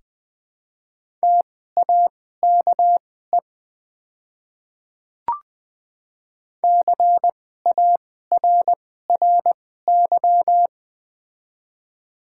Morse code, heard as a keyed tone, about 20 wpm, spaced as for 14 wpm, 700 Hz. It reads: TAKE CARRY
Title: 2x Character Spacing - Top 300 Words - 20wpm